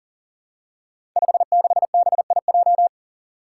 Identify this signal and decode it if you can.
Morse code, heard as a keyed tone, about 40 wpm, 700 Hz. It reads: 56BIJ
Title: Five Challenging Letter-Number Combinations 40wpm